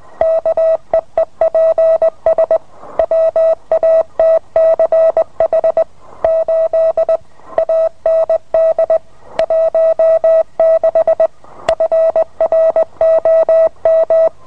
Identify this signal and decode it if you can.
Morse code, heard as a keyed tone, about 20 wpm, 650 Hz. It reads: KEEPS WATCH 8 AND 16 FROM